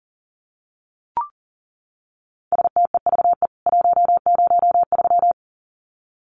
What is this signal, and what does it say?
Morse code, heard as a keyed tone, about 40 wpm, 700 Hz. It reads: STEVE 103
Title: CWT Exchange 40wpm